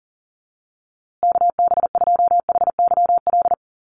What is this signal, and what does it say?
Morse code, heard as a keyed tone, about 40 wpm, 700 Hz. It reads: KB2HYL
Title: USA Call Signs Repeated 2x 40wpm